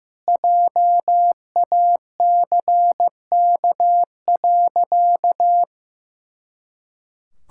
Morse code, heard as a keyed tone, about 16 wpm, 700 Hz. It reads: JACK.
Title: Pangram Challenge v2 - 15wpm